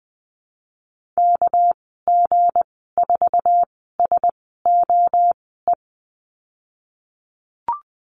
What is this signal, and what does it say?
Morse code, heard as a keyed tone, about 20 wpm, spaced as for 15 wpm, 700 Hz. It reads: KG4SOE
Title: 207 Lesson 33g - Letter j - Call Signs 20wpm